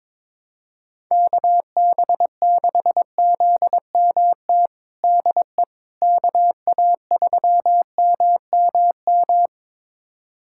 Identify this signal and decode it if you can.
Morse code, heard as a keyed tone, about 22 wpm, 700 Hz. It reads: KB6ZMT DE KA3MMM
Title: USA Callsign DE Callsign 22wpm